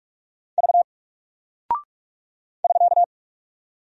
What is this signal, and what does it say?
Morse code, heard as a keyed tone, about 48 wpm, 700 Hz. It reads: V <SK>
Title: Single Character (Letter:Number:Punctuation:Prosign) Alt 45wpm